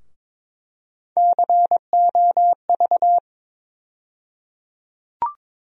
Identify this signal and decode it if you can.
Morse code, heard as a keyed tone, about 22 wpm, 700 Hz. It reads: COV
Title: USA Call Sign Postfixes - 3 Letters 22wpm